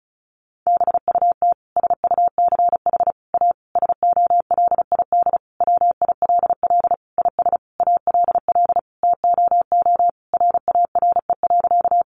Morse code, heard as a keyed tone, about 34 wpm, 700 Hz. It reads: BUT SUCH A SOLID WILL IS ALL TOO RARE.
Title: Sentences from Top 1500 Words 35wpm